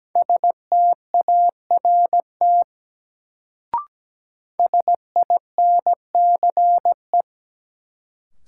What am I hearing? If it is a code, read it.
Morse code, heard as a keyed tone, about 17 wpm, 700 Hz. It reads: START SINCE
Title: Sentences from Top 300 Words - New Word Drill - 17wpm